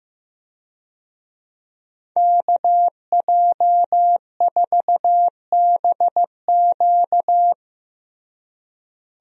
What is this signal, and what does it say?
Morse code, heard as a keyed tone, about 15 wpm, 700 Hz. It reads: KJ4BQ
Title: USA Call Signs - 5 Characters - Repeated 2x - 15wpm